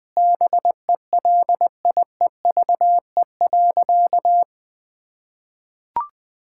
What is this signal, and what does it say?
Morse code, heard as a keyed tone, about 20 wpm, 700 Hz. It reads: BELIEVE.
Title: Sentences from Top 1000 Words - Review Part 6 - 20wpm